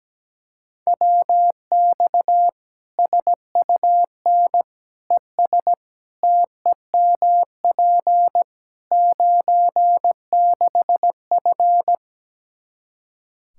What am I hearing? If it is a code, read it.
Morse code, heard as a keyed tone, about 17 wpm, 700 Hz. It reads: WX SUN ES TEMP 96F
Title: Practice QSO - Exchange 2 - Info - 17wpm